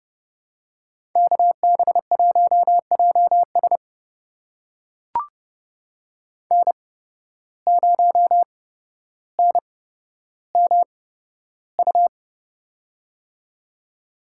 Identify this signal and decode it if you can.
Morse code, heard as a keyed tone, about 30 wpm, spaced as for 13 wpm, 700 Hz. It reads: KB1JS N0NMU
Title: USA Call Signs - 5 Characters - 8x Character Spacing - 30wpm